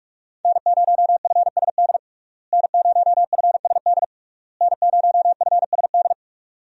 Morse code, heard as a keyed tone, about 44 wpm, 700 Hz. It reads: N0FSD N0FSD N0FSD
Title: USA Call Signs - 5 Characters - Repeated 3x - 45wpm